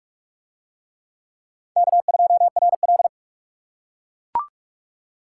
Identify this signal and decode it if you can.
Morse code, heard as a keyed tone, about 44 wpm, 700 Hz. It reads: K2RL